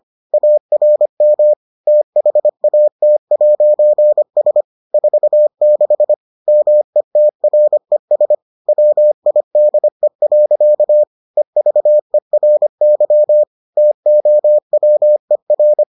Morse code, heard as a keyed tone, about 25 wpm, 600 Hz. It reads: ARM THAT'S 46 METRES WIDE. EVERY TOWER